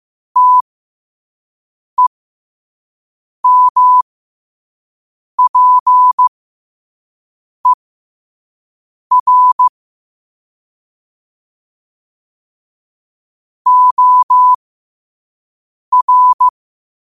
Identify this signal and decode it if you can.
Morse code, heard as a keyed tone, about 15 wpm, spaced as for 5 wpm, 1000 Hz. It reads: TEMPER OR
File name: qotdfiles / 5wpm